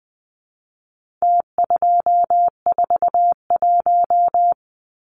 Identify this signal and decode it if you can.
Morse code, heard as a keyed tone, about 20 wpm, 700 Hz. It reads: T241
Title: Numbers - Contest Style - 4-Digit 20wpm